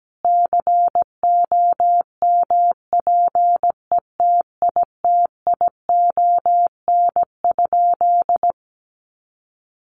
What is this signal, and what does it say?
Morse code, heard as a keyed tone, about 17 wpm, 700 Hz. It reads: COMPETITION?
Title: Sentences from Top 1000 Words 17wpm